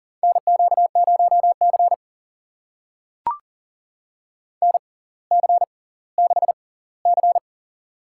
Morse code, heard as a keyed tone, about 39 wpm, spaced as for 20 wpm, 700 Hz. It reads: NQ0C NC6C
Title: USA Call Signs - 4 Characters - 6x Character Spacing - 40wpm